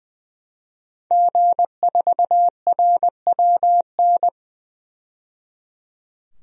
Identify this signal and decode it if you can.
Morse code, heard as a keyed tone, about 21 wpm, 700 Hz. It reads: G4RWN
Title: DXCC-100 Call Signs - 5 Characters - 20wpm